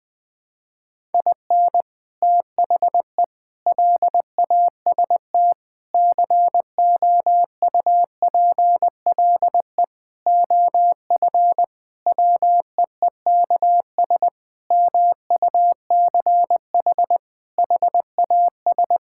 Morse code, heard as a keyed tone, about 20 wpm, 700 Hz. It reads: IN THE LAST COUPLE OF WEEKS MUCH HAS
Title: Sentences with 5-Word Phrases - 20wpm